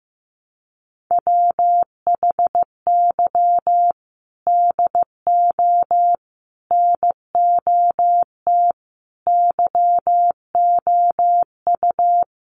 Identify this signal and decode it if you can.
Morse code, heard as a keyed tone, about 15 wpm, 700 Hz. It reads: WHY DO NOT YOU
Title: Sets of 5 Words Encore 15wpm